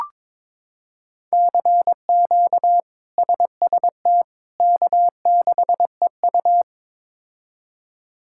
Morse code, heard as a keyed tone, about 22 wpm, 700 Hz. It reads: CQ SST K6EU
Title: K1USN SST QSO Practice - 22wpm